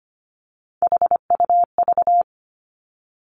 Morse code, heard as a keyed tone, about 25 wpm, 700 Hz. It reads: HUV
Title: Three Challenging Letter-Number Combinations 25wpm